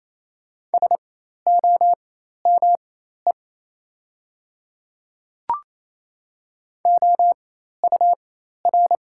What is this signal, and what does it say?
Morse code, heard as a keyed tone, about 27 wpm, spaced as for 13 wpm, 690 Hz. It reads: SOME OUR
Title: 4x Character Spacing - Top 200 Words - 28wpm